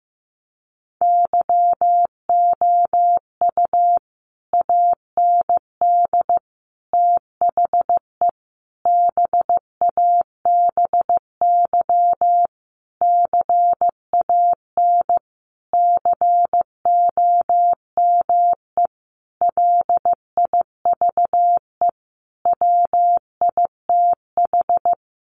Morse code, heard as a keyed tone, about 15 wpm, 700 Hz. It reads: YOU AND THE BABY CAN COME LIVE WITH
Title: Sentences from Top 1300 Words 15wpm